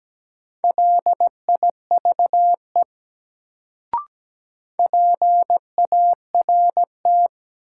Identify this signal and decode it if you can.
Morse code, heard as a keyed tone, about 17 wpm, 700 Hz. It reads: LIVE PART
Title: Sentences from Top 200 Words - New Word Drill - 17wpm